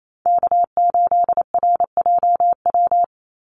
Morse code, heard as a keyed tone, about 28 wpm, 700 Hz. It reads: K8RJW
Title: Call Signs - 28wpm